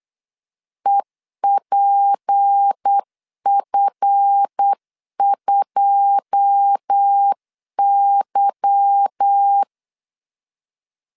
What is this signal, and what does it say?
Morse code, heard as a keyed tone, about 8 wpm, 800 Hz. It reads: EPF2Y
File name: LEZIONI / 9 286 8wpm 3425